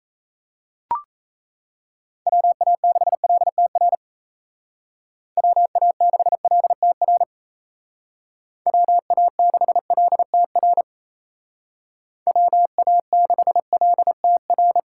Morse code, heard as a keyed tone, about 34 wpm, 700 Hz. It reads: WA6LTR WA6LTR WA6LTR WA6LTR
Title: Speed Racing - USA Call Signs 28wpm